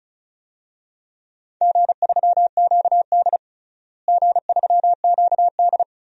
Morse code, heard as a keyed tone, about 34 wpm, 700 Hz. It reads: G3QD G3QD
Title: DXCC-100 Call Signs - 4 Characters - Repeated 3x - 35wpm